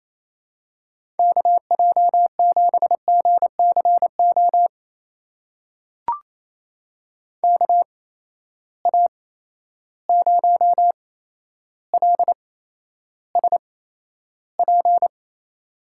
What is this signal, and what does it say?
Morse code, heard as a keyed tone, about 28 wpm, spaced as for 12 wpm, 700 Hz. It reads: KJ7GCO KA0LSP
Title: USA Call Signs - 6 Characters - 8x Character Spacing - 28wpm